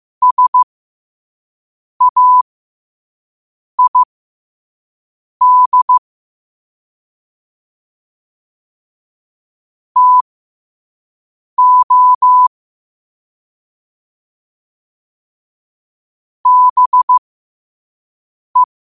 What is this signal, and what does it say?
Morse code, heard as a keyed tone, about 15 wpm, spaced as for 5 wpm, 1000 Hz. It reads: SAID TO BE
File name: qotdfiles / 5wpm